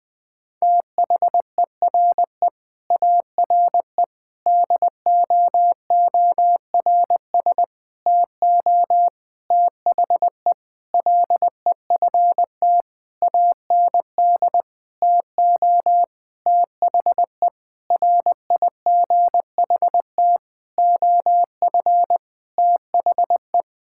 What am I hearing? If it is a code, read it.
Morse code, heard as a keyed tone, about 19 wpm, 700 Hz. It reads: THERE ARE DOORS TO THE LEFT AND TO THE RIGHT OF THE
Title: Sentences with 8-Word Phrases - 20wpm